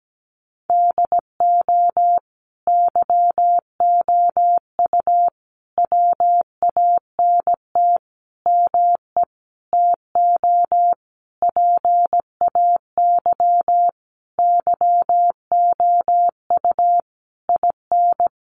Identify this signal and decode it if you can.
Morse code, heard as a keyed tone, about 17 wpm, 700 Hz. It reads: DO YOU WANT ME TO PAY YOU IN
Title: Sentences from Top 1400 Words 17wpm